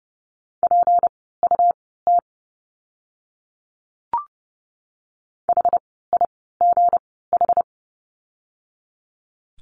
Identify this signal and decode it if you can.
Morse code, heard as a keyed tone, about 30 wpm, spaced as for 17 wpm, 700 Hz. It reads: PUT HIGH
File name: course / 30:10wpm / 237 Lesson 38f - Letter q - Words 30wpm